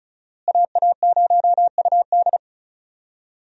Morse code, heard as a keyed tone, about 36 wpm, 700 Hz. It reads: AA0UD
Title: USA Call Signs - 5 Characters - Repeated 3x - 35wpm